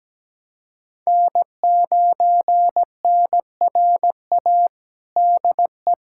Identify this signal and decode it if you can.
Morse code, heard as a keyed tone, about 17 wpm, 700 Hz. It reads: N9NRA DE
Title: USA Callsign DE Callsign 17wpm